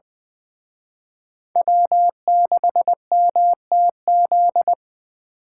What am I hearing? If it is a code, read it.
Morse code, heard as a keyed tone, about 25 wpm, 700 Hz. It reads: W6MTZ
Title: Speed Racing - USA 1x3 Call Signs 20wpm